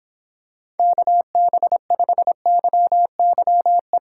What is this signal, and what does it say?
Morse code, heard as a keyed tone, about 26 wpm, 700 Hz. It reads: KB5YYE